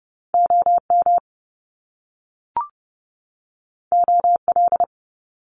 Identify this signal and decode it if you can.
Morse code, heard as a keyed tone, about 30 wpm, 700 Hz. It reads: O M O L
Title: Top Two Letter Combinations in Words 30wpm